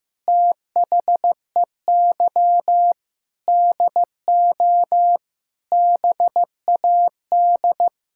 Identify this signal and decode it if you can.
Morse code, heard as a keyed tone, about 15 wpm, 700 Hz. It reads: THEY DO BAD
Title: Sentences from Top 1000 Words - Review Part 6 - 15wpm